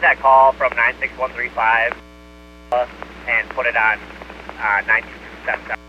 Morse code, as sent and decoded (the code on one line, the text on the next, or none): none